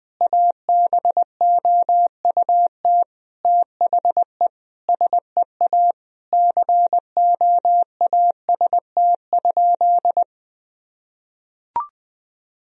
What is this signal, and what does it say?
Morse code, heard as a keyed tone, about 21 wpm, 700 Hz. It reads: ABOUT THE SEA COAST?
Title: Sentences from Top 1000 Words - Review Part 4 - 20wpm